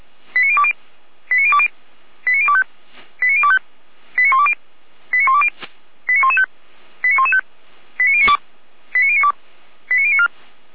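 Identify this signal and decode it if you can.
Five-tone selective call (ZVEI), heard as quick runs of five tones, tones of about 80 ms. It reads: 89020 89020 89025 89025 89120 89120 89115 89115 89092 89092 89094
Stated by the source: ZVEI five-tone call received by SDR.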